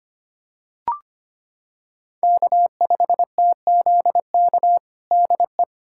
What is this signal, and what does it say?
Morse code, heard as a keyed tone, about 25 wpm, 700 Hz. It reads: K5TZK DE
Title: USA Callsign DE Callsign - 5 Characters - 25wpm